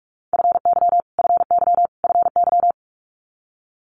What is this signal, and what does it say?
Morse code, heard as a keyed tone, about 46 wpm, 700 Hz. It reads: FY FY FY
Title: Multi-letter Phonemes Repeated 3x 45wpm